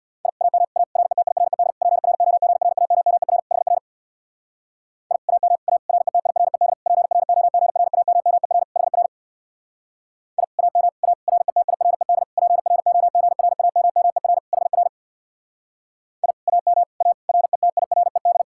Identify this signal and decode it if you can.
Morse code, heard as a keyed tone, about 59 wpm, 700 Hz. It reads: I AM A RETIRED PROGRAMMER HR I AM A RETIRED PROGRAMMER HR I AM A RETIRED PROGRAMMER HR I AM A RETIRED
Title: Speed-Racing Common QSO Phrases Encore - 50wpm